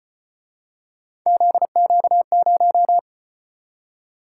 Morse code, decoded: ZQ0

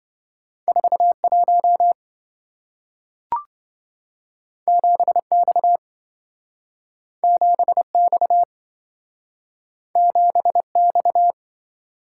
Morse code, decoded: 41 7X 7X 7X

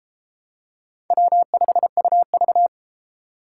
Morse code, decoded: W5UV

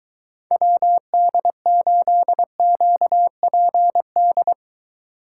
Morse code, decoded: WD8QPD

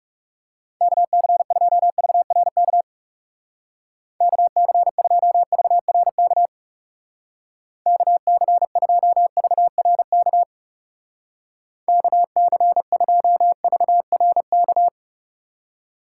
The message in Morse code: KC2VRK KC2VRK KC2VRK KC2VRK

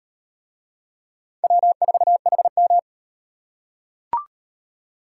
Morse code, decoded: W4HM

